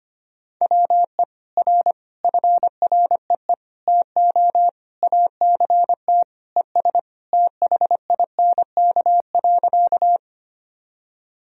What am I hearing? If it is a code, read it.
Morse code, heard as a keyed tone, about 25 wpm, 700 Hz. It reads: WE R FREE TO ACT ES THINK.
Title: Sentences from Top 500 Words and Abbreviations 25wpm